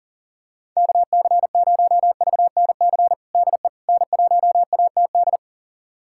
Morse code, heard as a keyed tone, about 40 wpm, 700 Hz. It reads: KC0VNC DE N1ATD
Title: USA Callsign DE Callsign 40wpm